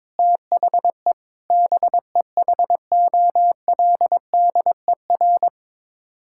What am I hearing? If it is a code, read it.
Morse code, heard as a keyed tone, about 22 wpm, 700 Hz. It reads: THE BEHOLDER
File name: practice / 22wpm / Sets of 6 Words - 22wpm